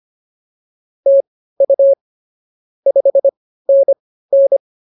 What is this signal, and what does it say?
Morse code, heard as a keyed tone, about 24 wpm, spaced as for 15 wpm, 550 Hz. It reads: TU 5NN